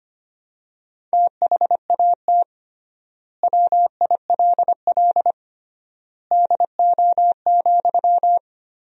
Morse code, heard as a keyed tone, about 25 wpm, 700 Hz. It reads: THAT WILL DO,
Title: 3x Word-Spacing - Sentences from Top 100 Words - 25wpm